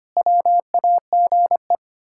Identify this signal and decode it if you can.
Morse code, heard as a keyed tone, about 25 wpm, 700 Hz. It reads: WAGE
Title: Pangram Challenge v2 - 25wpm